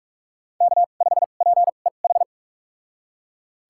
Morse code, heard as a keyed tone, about 41 wpm, spaced as for 31 wpm, 700 Hz. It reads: K5PEH
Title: USA Call Signs - 2x Character Spacing - 45wpm